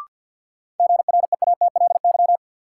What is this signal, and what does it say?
Morse code, heard as a keyed tone, about 53 wpm, 700 Hz. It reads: GREATLY
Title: Sentences from Top 1400 Words - New Word Drill - 50wpm